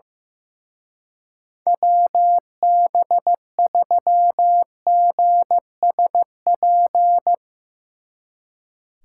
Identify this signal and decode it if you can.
Morse code, heard as a keyed tone, about 17 wpm, 700 Hz. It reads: WB3GSP
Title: Speed Racing - USA Call Signs 15wpm